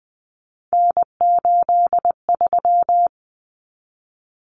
Morse code, heard as a keyed tone, about 20 wpm, 700 Hz. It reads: N83
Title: Common Cut Numbers - 3-Digit 20wpm